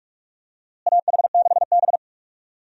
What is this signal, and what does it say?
Morse code, heard as a keyed tone, about 46 wpm, 700 Hz. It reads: AH6B
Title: USA Call Signs - 4 Characters - Repeated 2x - 45wpm